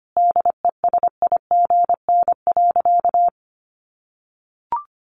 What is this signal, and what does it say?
Morse code, heard as a keyed tone, about 25 wpm, 700 Hz. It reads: DESIGN.
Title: Sentences from Top 800 Words 25wpm